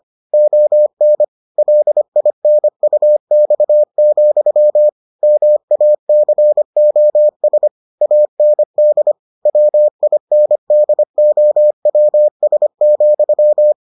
Morse code, heard as a keyed tone, about 25 wpm, 600 Hz. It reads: ON LINUX, MACOS AND WINDOWS,